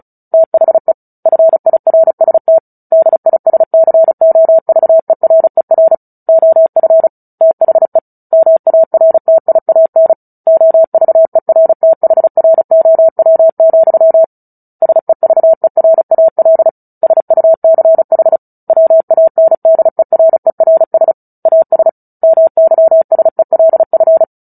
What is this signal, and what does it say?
Morse code, heard as a keyed tone, about 34 wpm, 650 Hz. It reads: THE FIRST DISCOVERER OF THE MARTIAN OVERTHROW, SEVERAL SUCH WANDERERS AS MYSELF
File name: Book Two / 09-Chapter 9 - 35 WPM